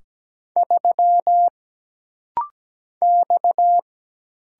Morse code, heard as a keyed tone, about 17 wpm, 700 Hz. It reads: 3 X
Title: Single Letter-Number Alt - Rapid-Fire 17wpm